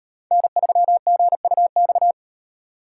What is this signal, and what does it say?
Morse code, heard as a keyed tone, about 38 wpm, 700 Hz. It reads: N3GUX